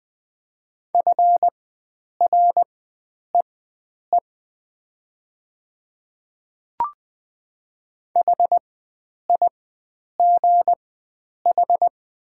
Morse code, heard as a keyed tone, about 20 wpm, spaced as for 9 wpm, 700 Hz. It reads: FREE HIGH
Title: 4x Character Spacing - Top 500 Words - 20wpm